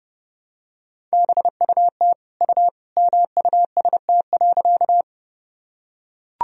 Morse code, heard as a keyed tone, about 30 wpm, 700 Hz. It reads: BUT U MUST.
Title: Sentences from Top 200 Words and Abbreviations 30wpm